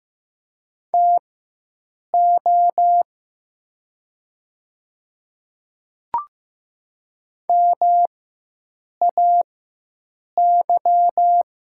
Morse code, heard as a keyed tone, about 15 wpm, spaced as for 7 wpm, 700 Hz. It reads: TO MAY
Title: 4x Character Spacing - Top 100 Words - 15wpm